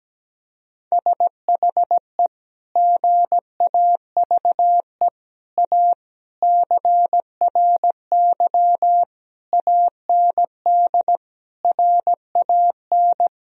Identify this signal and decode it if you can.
Morse code, heard as a keyed tone, about 17 wpm, 700 Hz. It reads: SHE GAVE A CRY AND RAN
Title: Sentences from Top 1000 Words - Review Part 3 - 17wpm